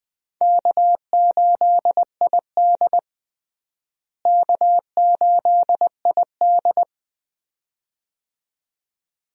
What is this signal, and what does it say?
Morse code, heard as a keyed tone, about 20 wpm, 700 Hz. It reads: K8ID K8ID
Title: USA Call Signs - 4 Characters - Repeated 2x - 20wpm